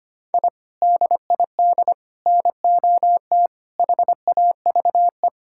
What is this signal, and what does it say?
Morse code, heard as a keyed tone, about 25 wpm, 700 Hz. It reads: I DID NOT HAVE
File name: practice / 25wpm / Sentences with 3-Word Phrases - 25wpm